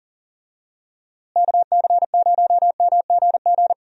Morse code, heard as a keyed tone, about 40 wpm, 700 Hz. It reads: KC0MGG
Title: USA Call Signs - 6 Characters - Repeated 2x - 40wpm